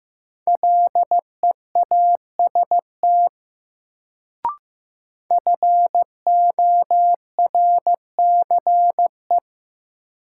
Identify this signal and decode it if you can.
Morse code, heard as a keyed tone, about 15 wpm, 700 Hz. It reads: LEAST FORCE
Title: Sentences from Top 700 Words - New Word Drill - 15wpm